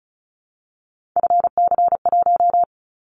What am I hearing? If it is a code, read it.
Morse code, heard as a keyed tone, about 36 wpm, 700 Hz. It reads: FC1
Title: Callsign Prefixes for Top 25 DXCC - 35wpm